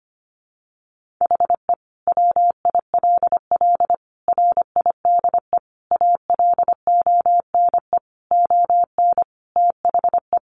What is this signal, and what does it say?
Morse code, heard as a keyed tone, about 25 wpm, 700 Hz. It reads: HE WILL RIDE ALONE ON THE
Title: Sentences from Top 1400 Words 25wpm